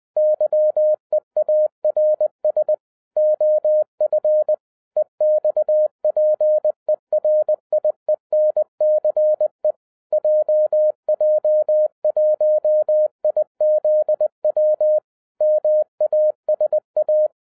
Morse code, heard as a keyed tone, about 20 wpm, 610 Hz. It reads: YEARS OF EXPERIENCE JJ1IZW MASA
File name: 2011-10-mp3 / n13-TIPS FROM MY 34 YEARS OF EXPERIENCE 20wpm